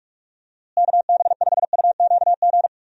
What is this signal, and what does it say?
Morse code, heard as a keyed tone, about 46 wpm, 700 Hz. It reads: KB5UQG
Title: USA Call Signs - 6 Characters - Repeated 3x - 45wpm